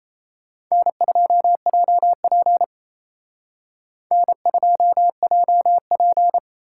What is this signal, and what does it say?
Morse code, heard as a keyed tone, about 32 wpm, 700 Hz. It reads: N2JP N2JP
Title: Speed Racing - USA 1x2 Call Signs 28wpm